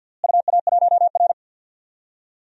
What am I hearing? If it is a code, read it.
Morse code, heard as a keyed tone, about 51 wpm, 700 Hz. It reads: UA1R
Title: DXCC-100 Call Signs - 4 Characters - Repeated 3x - 50wpm